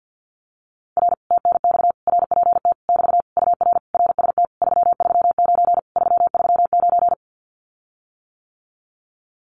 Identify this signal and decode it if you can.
Morse code, heard as a keyed tone, about 50 wpm, 700 Hz. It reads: R TNX RPT <BT> UR RST 339 339
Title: Common QSO Phrases Encore 50wpm